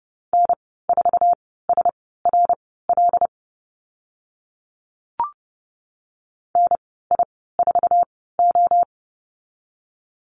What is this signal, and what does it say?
Morse code, heard as a keyed tone, about 30 wpm, spaced as for 17 wpm, 700 Hz. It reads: N4SRL NI4O
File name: Lesson 15g - Number 5 - Call Signs 30wpm